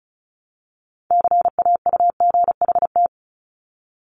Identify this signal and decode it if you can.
Morse code, heard as a keyed tone, about 36 wpm, 700 Hz. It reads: CAUGHT
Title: Multi-letter Phonemes 35wpm